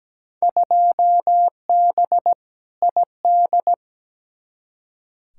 Morse code, heard as a keyed tone, about 17 wpm, 700 Hz. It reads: 2B ID
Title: Field Day Exchanges Encore - 17wpm